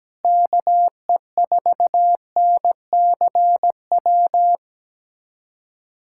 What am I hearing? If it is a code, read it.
Morse code, heard as a keyed tone, about 17 wpm, 700 Hz. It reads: KE4NCW
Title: USA Callsign DE Callsign - 6 Characters - 17wpm